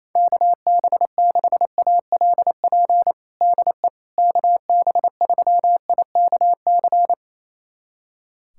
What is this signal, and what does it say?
Morse code, heard as a keyed tone, about 28 wpm, 700 Hz. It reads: KB6ALP DE KB3IKC
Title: USA Callsign DE Callsign 28wpm